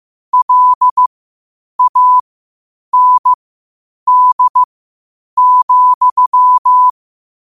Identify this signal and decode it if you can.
Morse code, heard as a keyed tone, about 15 wpm, spaced as for 8 wpm, 1000 Hz. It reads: LAND,
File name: qotdfiles / 8wpm